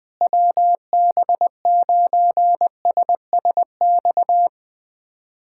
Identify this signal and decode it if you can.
Morse code, heard as a keyed tone, about 19 wpm, 700 Hz. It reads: WB9SSX
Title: USA Callsign DE Callsign - 6 Characters - 20wpm